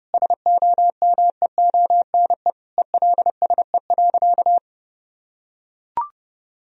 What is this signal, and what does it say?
Morse code, heard as a keyed tone, about 32 wpm, 700 Hz. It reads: SOMEONE ELSE.
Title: Sentences from Top 1000 Words - Review Part 4 - 30wpm